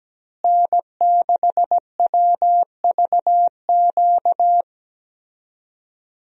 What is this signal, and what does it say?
Morse code, heard as a keyed tone, about 17 wpm, 700 Hz. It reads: N6WVQ
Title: USA Call Signs - 5 Characters - Repeated 3x - 17wpm